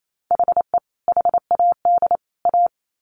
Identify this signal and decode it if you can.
Morse code, heard as a keyed tone, about 28 wpm, 700 Hz. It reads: HE HAD A
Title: Sets of 3 Words 28wpm